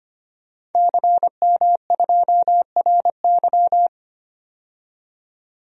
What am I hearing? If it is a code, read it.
Morse code, heard as a keyed tone, about 25 wpm, 700 Hz. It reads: CM2RY
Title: DXCC-100 Call Signs - 5 Characters - 25wpm